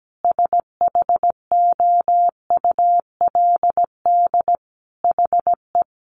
Morse code, heard as a keyed tone, about 17 wpm, 700 Hz. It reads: SHOULD HE
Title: Sentences from Top 300 Words 17wpm